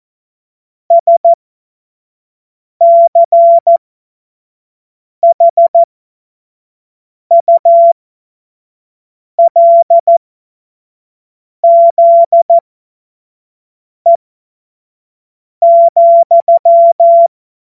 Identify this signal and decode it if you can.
Morse code, heard as a keyed tone, about 14 wpm, spaced as for 5 wpm, 680 Hz. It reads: SCHULZE,
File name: Morsekurs / Witze / Witz 14. März 2023 25 F70